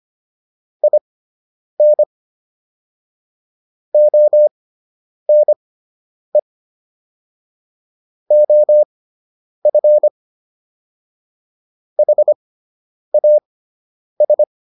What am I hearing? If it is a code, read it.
Morse code, heard as a keyed tone, about 25 wpm, spaced as for 9 wpm, 600 Hz. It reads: IN ONE OF HAS